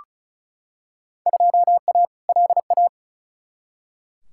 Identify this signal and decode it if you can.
Morse code, heard as a keyed tone, about 34 wpm, 700 Hz. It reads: 2A LA
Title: Field Day Exchanges - 35wpm